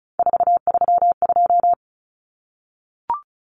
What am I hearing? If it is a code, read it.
Morse code, heard as a keyed tone, about 34 wpm, 700 Hz. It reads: 432
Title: Numbers - 3-Digit 35wpm